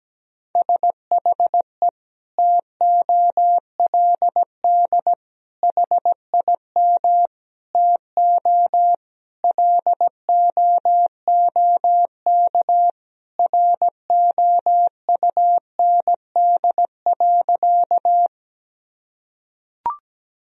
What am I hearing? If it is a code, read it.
Morse code, heard as a keyed tone, about 17 wpm, 700 Hz. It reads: SHE TOLD HIM TO LOOK ROUND.
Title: Sentences from Top 1000 Words - Review Part 1 - 17wpm